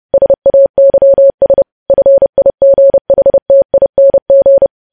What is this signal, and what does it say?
Morse code, heard as a keyed tone, about 30 wpm, 550 Hz. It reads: SAYS FIGHTING
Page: news headlines in morse code at 30 wpm